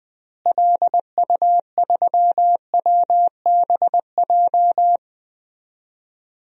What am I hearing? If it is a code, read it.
Morse code, heard as a keyed tone, about 21 wpm, 700 Hz. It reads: LU3WBJ
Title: DXCC-100 Call Signs - 6 Characters - Repeated 3x - 20wpm